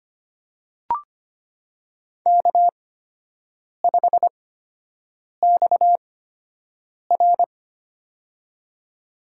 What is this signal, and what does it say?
Morse code, heard as a keyed tone, about 25 wpm, spaced as for 7 wpm, 700 Hz. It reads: K5XR
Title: USA Call Signs - 4 Characters - 8x Character Spacing - 25wpm